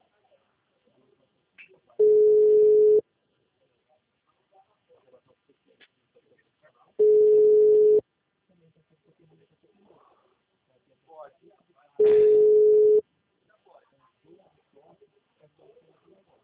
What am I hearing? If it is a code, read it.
Ringback tone (Europe, 425 Hz), with speech.
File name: out- -2016